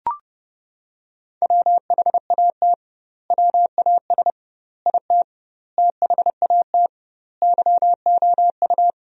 Morse code, heard as a keyed tone, about 30 wpm, 700 Hz. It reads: WHAT WAS IT THAT YOU
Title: 2x Word-Spacing - Sentences from Top 300 Words - 30wpm